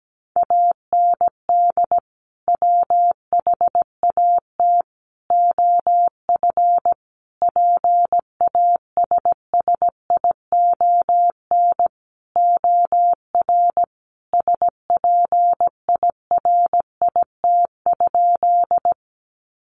Morse code, heard as a keyed tone, about 17 wpm, 700 Hz. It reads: AND WHAT OF PASSION OR SPIRIT?
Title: Sentences from Top 1400 Words 17wpm